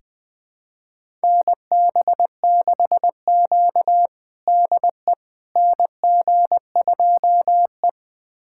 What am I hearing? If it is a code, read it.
Morse code, heard as a keyed tone, about 20 wpm, 700 Hz. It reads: NB6Q DE NG2E
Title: USA Callsign DE Callsign - 4 Characters - 20wpm